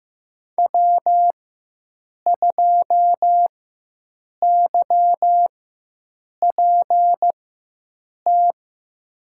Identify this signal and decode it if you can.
Morse code, heard as a keyed tone, about 15 wpm, spaced as for 7 wpm, 700 Hz. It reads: W2YPT
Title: USA Call Signs - 5 Characters - 4x Character Spacing - 15wpm